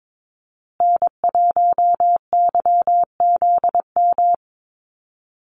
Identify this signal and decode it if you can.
Morse code, heard as a keyed tone, about 22 wpm, 700 Hz. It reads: N1YZM
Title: USA Call Signs Repeated 3x 22wpm